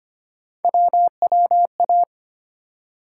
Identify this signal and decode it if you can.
Morse code, heard as a keyed tone, about 25 wpm, 700 Hz. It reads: WWA
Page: morse-code-ninja-practice-25wpm